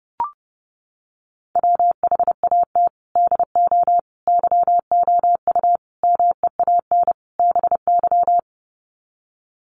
Morse code, heard as a keyed tone, about 29 wpm, 700 Hz. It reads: WHAT DO YOU MEAN BY
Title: Sets of 5 Words 30wpm